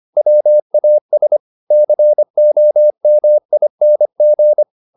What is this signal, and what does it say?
Morse code, heard as a keyed tone, about 25 wpm, 600 Hz. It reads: WAS COMING
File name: the old man and the sea part 1 25wpm